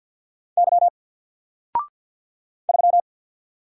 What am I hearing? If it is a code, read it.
Morse code, heard as a keyed tone, about 48 wpm, 700 Hz. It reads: Y 3
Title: Single Character (Letter:Number:Punctuation:Prosign) Alt 50wpm